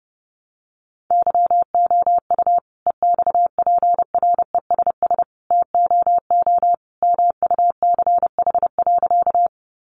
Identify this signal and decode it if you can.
Morse code, heard as a keyed tone, about 30 wpm, 700 Hz. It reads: YOU EXPRESS TOO MUCH.
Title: Sentences from Top 1100 Words 30wpm